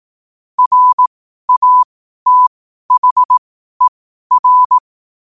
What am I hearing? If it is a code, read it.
Morse code, heard as a keyed tone, about 18 wpm, spaced as for 12 wpm, 1000 Hz. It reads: RATHER
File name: qotdfiles / 12wpm